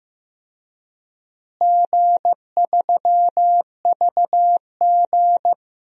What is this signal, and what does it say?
Morse code, heard as a keyed tone, about 15 wpm, 700 Hz. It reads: G3VG
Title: DXCC-100 Call Signs - 4 Characters - Repeated 3x - 15wpm